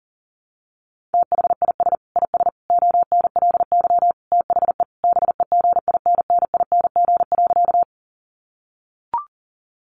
Morse code, heard as a keyed tone, about 40 wpm, 700 Hz. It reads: THIS IS ONLY THE BEGINNING.
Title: Sentences from Top 800 Words 40wpm